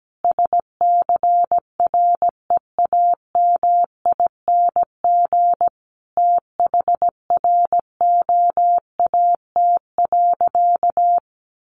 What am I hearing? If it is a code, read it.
Morse code, heard as a keyed tone, about 17 wpm, 700 Hz. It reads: SCREAMING THROAT.
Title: A Sound Of Thunder 17wpm